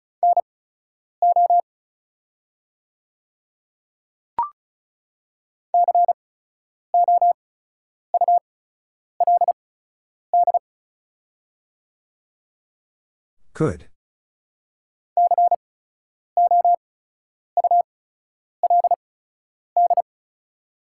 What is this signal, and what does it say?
Morse code, heard as a keyed tone, about 36 wpm, spaced as for 10 wpm, 690 Hz. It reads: NO COULD COULD
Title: 8x Character Spacing - Top 100 Words - 35wpm